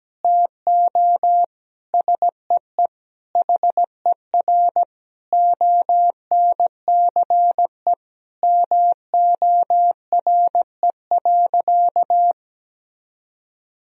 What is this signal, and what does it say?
Morse code, heard as a keyed tone, about 17 wpm, 700 Hz. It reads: TO SEE HER ONCE MORE.
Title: Sentences from Top 500 Words and Abbreviations 17wpm